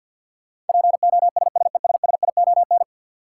Morse code, heard as a keyed tone, about 50 wpm, 700 Hz. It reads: POSSESSION
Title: Sentences from Top 1400 Words - New Word Drill - 50wpm